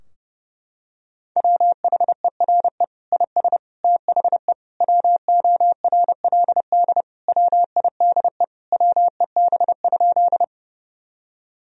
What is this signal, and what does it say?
Morse code, heard as a keyed tone, about 30 wpm, 700 Hz. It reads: WHERE IS THE WORLD WIDE WEB?
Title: Sentences from Top 1000 Words - Review Part 4 - 30wpm